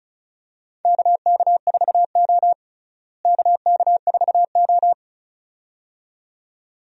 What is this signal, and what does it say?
Morse code, heard as a keyed tone, about 36 wpm, 700 Hz. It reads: KK4O KK4O
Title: USA Call Signs - 4 Characters - Repeated 2x - 35wpm